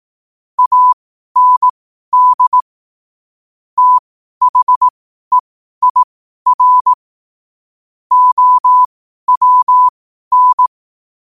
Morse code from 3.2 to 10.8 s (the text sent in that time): THEIR OWN